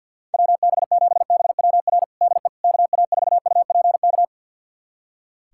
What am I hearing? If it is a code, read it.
Morse code, heard as a keyed tone, about 45 wpm, 700 Hz. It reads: WB7BWR DE KA4UPK
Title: USA Callsign DE Callsign 50wpm